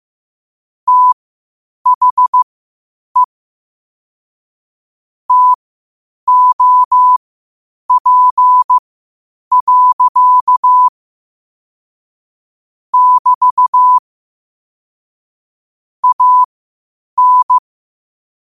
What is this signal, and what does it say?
Morse code, heard as a keyed tone, about 15 wpm, spaced as for 8 wpm, 1000 Hz. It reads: THE TOP. <BT> AN
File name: qotdfiles / 8wpm